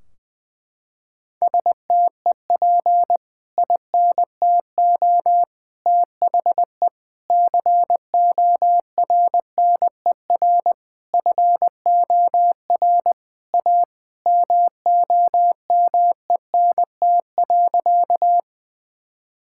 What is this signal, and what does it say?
Morse code, heard as a keyed tone, about 20 wpm, 700 Hz. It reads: STEP INTO THE CORNER FOR A MOMENT.
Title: Sentences from Top 1600 Words 20wpm